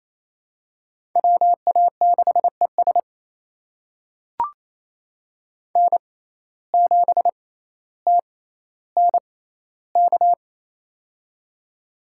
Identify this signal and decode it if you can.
Morse code, heard as a keyed tone, about 28 wpm, spaced as for 14 wpm, 700 Hz. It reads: WA6ES N7TNK
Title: USA Call Signs - 6x Character Spacing - 28wpm